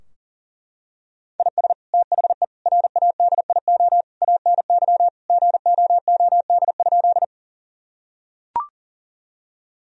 Morse code, heard as a keyed tone, about 40 wpm, 700 Hz. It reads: IS THE RADIO ANY GOOD?
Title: Sentences from Top 1000 Words - Review Part 3 - 40wpm